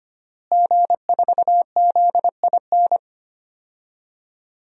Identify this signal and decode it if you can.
Morse code, heard as a keyed tone, about 25 wpm, 700 Hz. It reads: G4ZIN
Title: DXCC-100 Call Signs - 5 Characters - Repeated 2x - 25wpm